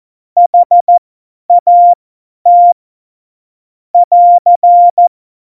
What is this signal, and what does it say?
Morse code, heard as a keyed tone, about 14 wpm, spaced as for 10 wpm, 700 Hz. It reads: HAT <AR>
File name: Morsekurs / Witze / Witz 11. Juni 2019, 50 F70